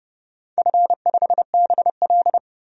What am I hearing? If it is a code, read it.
Morse code, heard as a keyed tone, about 31 wpm, 700 Hz. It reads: F5BL
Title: DXCC-100 Call Signs - 4 Characters - 30wpm